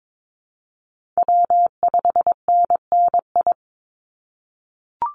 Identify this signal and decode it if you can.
Morse code, heard as a keyed tone, about 22 wpm, 700 Hz. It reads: W5NNI
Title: USA Call Signs 22wpm